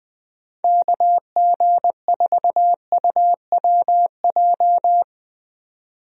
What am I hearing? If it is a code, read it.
Morse code, heard as a keyed tone, about 20 wpm, 700 Hz. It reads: KG4UWJ
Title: USA Call Signs - 6 Characters - 20wpm